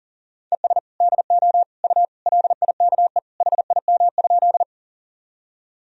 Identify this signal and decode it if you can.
Morse code, heard as a keyed tone, about 37 wpm, 700 Hz. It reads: ES DO U LIKE HIM?
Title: Sentences from Top 100 Words and Abbreviations 40wpm